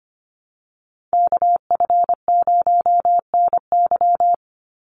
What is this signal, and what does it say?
Morse code, heard as a keyed tone, about 25 wpm, 700 Hz. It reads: KF0NY
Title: USA Call Signs 25wpm